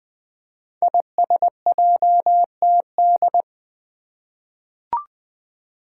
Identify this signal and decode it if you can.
Morse code, heard as a keyed tone, about 20 wpm, 700 Hz. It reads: ISJTD
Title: Five Challenging Letter-Number Combinations 20wpm